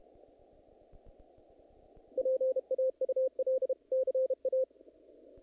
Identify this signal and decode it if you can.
Morse code, heard as a keyed tone, about 32 wpm, 520 Hz. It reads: PAUL CA
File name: contests / naqp-cw-k5tr-op-k5pi- -1900